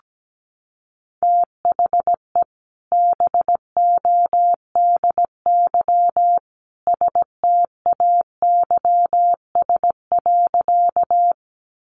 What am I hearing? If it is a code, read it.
Morse code, heard as a keyed tone, about 17 wpm, 700 Hz. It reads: THE BODY STAYS.
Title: A Sound Of Thunder 17wpm